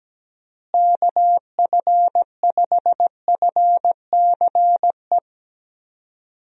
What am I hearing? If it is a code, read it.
Morse code, heard as a keyed tone, about 17 wpm, 700 Hz. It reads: KF5FCE
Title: USA Call Signs - 6 Characters - Repeated 2x - 17wpm